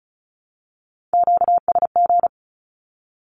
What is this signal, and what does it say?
Morse code, heard as a keyed tone, about 34 wpm, 700 Hz. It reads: QSG
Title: Q-Codes 35wpm